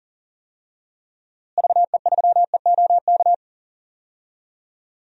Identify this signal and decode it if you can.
Morse code, heard as a keyed tone, about 40 wpm, 700 Hz. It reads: VE3EOK
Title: DXCC-100 Call Signs - 6 Characters - Repeated 3x - 40wpm